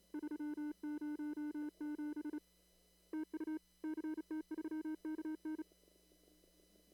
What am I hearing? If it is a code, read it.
Morse code, heard as a keyed tone, about 34 wpm, 310 Hz. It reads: 307 TU CT3KN
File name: CW / kt5j wpx cw-op-k5pi